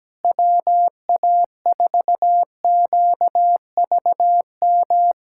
Morse code, heard as a keyed tone, about 17 wpm, 700 Hz. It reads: WA4QVM
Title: USA Callsign DE Callsign - 6 Characters - 17wpm